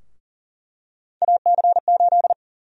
Morse code, heard as a keyed tone, about 40 wpm, 700 Hz. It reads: AC8
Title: USA Call Sign Prefixes - Repeated 2x - 40wpm